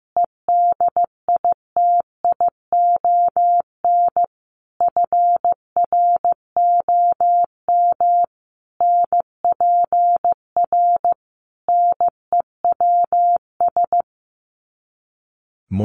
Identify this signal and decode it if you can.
Morse code, heard as a keyed tone, about 15 wpm, 700 Hz. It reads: EDITION FROM NPR NEWS
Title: Sets of 5 Words Encore 15wpm